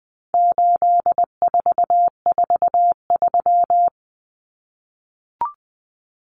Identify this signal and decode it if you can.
Morse code, heard as a keyed tone, about 20 wpm, 700 Hz. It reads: 8443
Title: Common Cut Numbers - 4-Digit 20wpm